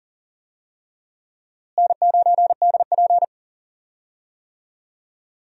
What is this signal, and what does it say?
Morse code, heard as a keyed tone, about 40 wpm, 700 Hz. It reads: N9DP
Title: USA Call Signs - 4 Characters - Repeated 3x - 40wpm